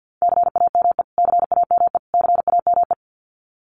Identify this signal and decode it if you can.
Morse code, heard as a keyed tone, about 52 wpm, 700 Hz. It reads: CAME CAME CAME
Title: Top 200 Words Repeated 50wpm